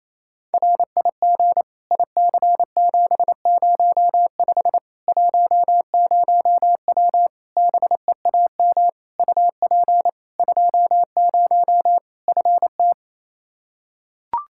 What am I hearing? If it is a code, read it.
Morse code, heard as a keyed tone, about 28 wpm, 700 Hz. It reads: RIG IC705 10W BEAM UP 20 FT
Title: Practice QSO - Exchange 2 - Info - 28wpm